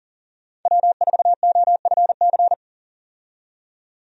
Morse code, heard as a keyed tone, about 40 wpm, 700 Hz. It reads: W4OFC
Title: USA Call Signs - 5 Characters - 40wpm